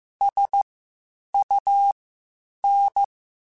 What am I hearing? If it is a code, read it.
Morse code, heard as a keyed tone, about 15 wpm, spaced as for 8 wpm, 800 Hz. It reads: SUN
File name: Spaced Characters / Spaced Characters-mp3s / Weather 5 12 sp @13